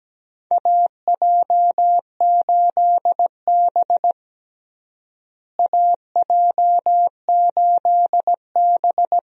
Morse code, heard as a keyed tone, about 17 wpm, 700 Hz. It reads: AJ8B AJ8B
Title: USA Call Signs - 4 Characters - Repeated 2x - 17wpm